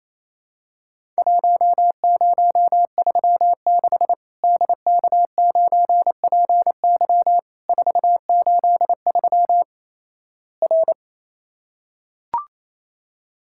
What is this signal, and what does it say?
Morse code, heard as a keyed tone, about 28 wpm, 700 Hz. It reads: 1036 DK9PY 483 R
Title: Worked All Europe DX Contest - QTC Exchange - 28wpm